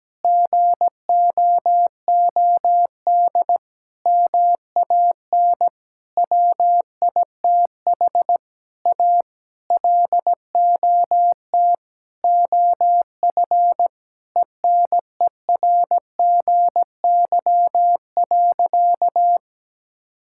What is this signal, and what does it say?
Morse code, heard as a keyed tone, about 17 wpm, 700 Hz. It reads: GOOD MAN WITH A LOT OF ENERGY.
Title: Sentences from Top 1000 Words - Review Part 6 - 17wpm